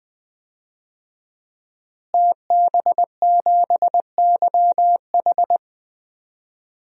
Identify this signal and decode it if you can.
Morse code, heard as a keyed tone, about 20 wpm, 700 Hz. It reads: TB7YH
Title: DXCC-100 Call Signs - 5 Characters - Repeated 3x - 20wpm